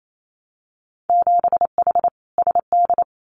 Morse code, heard as a keyed tone, about 28 wpm, 690 Hz. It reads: 7H SD
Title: Winter Field Day Exchanges Encore 28wpm